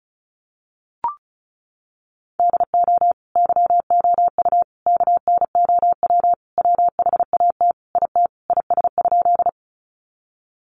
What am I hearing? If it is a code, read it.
Morse code, heard as a keyed tone, about 34 wpm, 700 Hz. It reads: DO YOU KNOW WHAT IT IS?
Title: Sentences from Top 100 Words 35wpm